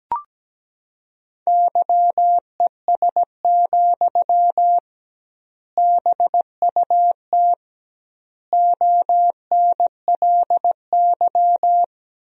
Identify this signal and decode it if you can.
Morse code, heard as a keyed tone, about 17 wpm, 700 Hz. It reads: YES, BUT ONLY
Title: 2x Word-Spacing - Sentences from Top 500 Words - 17wpm